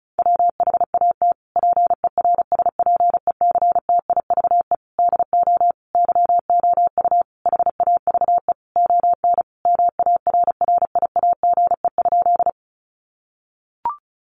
Morse code, heard as a keyed tone, about 34 wpm, 700 Hz. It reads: WHAT PERSPECTIVE DO YOU HAVE ON MARRIAGE?
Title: Sentences from Top 1200 Words 35wpm